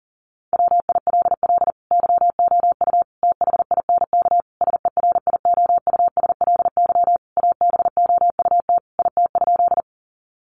Morse code, heard as a keyed tone, about 40 wpm, 700 Hz. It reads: WILL YOU THINK SERIOUSLY ABOUT IT?
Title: Sentences from Top 1200 Words 40wpm